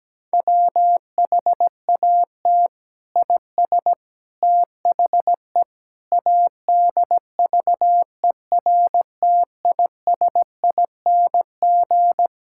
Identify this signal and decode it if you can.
Morse code, heard as a keyed tone, about 17 wpm, 700 Hz. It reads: WHAT IS THE ADVERTISING
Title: Sentences from Top 1000 Words - Review Part 6 - 17wpm